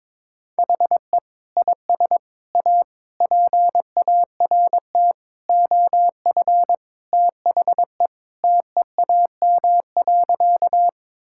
Morse code, heard as a keyed tone, about 22 wpm, 700 Hz. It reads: HE IS A PART OF THE TEAM.
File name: practice / 22wpm / Sentences with 4-Word Phrases - 22wpm